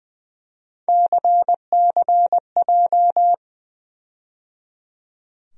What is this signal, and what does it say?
Morse code, heard as a keyed tone, about 20 wpm, 700 Hz. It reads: CCJ
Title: USA Call Sign Postfixes - 3 Letters 20wpm